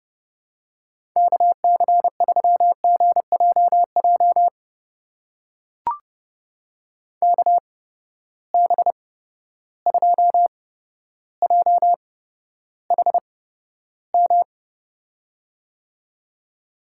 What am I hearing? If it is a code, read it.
Morse code, heard as a keyed tone, about 32 wpm, spaced as for 13 wpm, 700 Hz. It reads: KC3GJJ KB2JHM
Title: USA Call Signs - 6 Characters - 8x Character Spacing - 30wpm